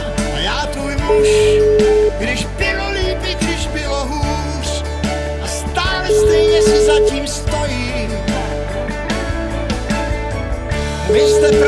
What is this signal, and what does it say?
Ringback tone (Europe, 425 Hz), with speech.